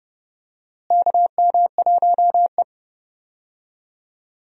Morse code, heard as a keyed tone, about 32 wpm, 700 Hz. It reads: KM1E
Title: USA Call Signs - 4 Characters - Repeated 2x - 30wpm